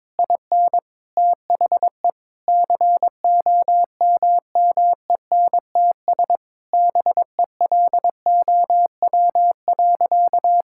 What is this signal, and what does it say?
Morse code, heard as a keyed tone, about 22 wpm, 700 Hz. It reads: IN THE COMMENTS BELOW.
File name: Sentences with 7-Word Phrases - 22wpm